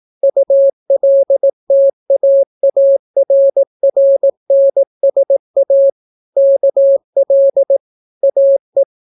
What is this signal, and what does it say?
Morse code, heard as a keyed tone, about 18 wpm, 550 Hz. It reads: ULTAARRNSA KL AE